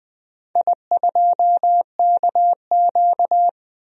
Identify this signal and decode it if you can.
Morse code, heard as a keyed tone, about 20 wpm, 700 Hz. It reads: I2KQ
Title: DXCC-100 Call Signs - 4 Characters - 20wpm